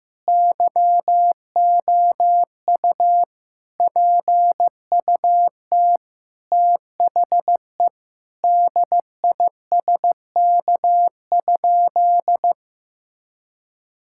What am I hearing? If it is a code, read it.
Morse code, heard as a keyed tone, about 15 wpm, 700 Hz. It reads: YOU PUT THE DISK?
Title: Sentences from Top 1000 Words - Review Part 6 - 15wpm